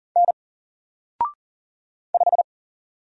Morse code, heard as a keyed tone, about 40 wpm, 700 Hz. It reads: N 5
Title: Single Letter-Number 40wpm